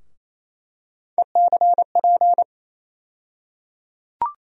Morse code, heard as a keyed tone, about 28 wpm, 700 Hz. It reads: ECP
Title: USA Call Sign Postfixes - 3 Letters 28wpm